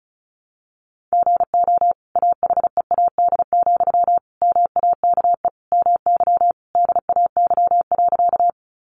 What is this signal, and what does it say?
Morse code, heard as a keyed tone, about 34 wpm, 700 Hz. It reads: GO AHEAD, MAKE MY DAY.
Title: Sentences from Top 1100 Words 35wpm